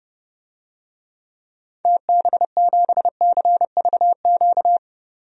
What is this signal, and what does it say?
Morse code, heard as a keyed tone, about 30 wpm, 700 Hz. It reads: TB7CVQ
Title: DXCC-100 Call Signs - 6 Characters - 30wpm